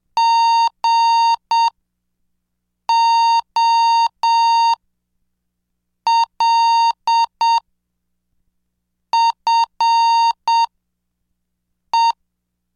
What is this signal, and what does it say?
Morse code, heard as a keyed tone, about 7 wpm, spaced as for 4 wpm, 920 Hz. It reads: GOLFE